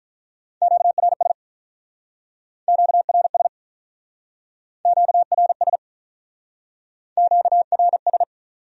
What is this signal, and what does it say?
Morse code, heard as a keyed tone, about 44 wpm, 700 Hz. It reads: QRS QRS QRS QRS
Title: Speed-Racing Q-Codes - 35wpm